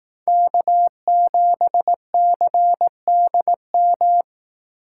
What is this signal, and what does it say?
Morse code, heard as a keyed tone, about 18 wpm, 700 Hz. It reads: K7CDM
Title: Speed Racing - USA 1x3 Call Signs 15wpm